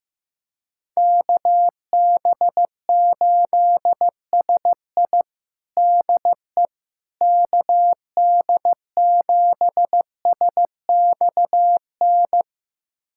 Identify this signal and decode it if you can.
Morse code, heard as a keyed tone, about 16 wpm, 700 Hz. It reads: KB8SI DE KD7SXN
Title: USA Callsign DE Callsign 15wpm